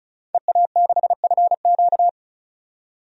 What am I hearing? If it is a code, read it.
Morse code, heard as a keyed tone, about 36 wpm, 700 Hz. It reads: EA6FQ
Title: DXCC-100 Call Signs - 5 Characters - 35wpm